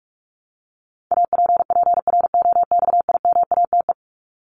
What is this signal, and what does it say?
Morse code, heard as a keyed tone, about 44 wpm, 700 Hz. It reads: APPROXIMATE
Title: LY Suffix - Top 100 Words with Base Words 45wpm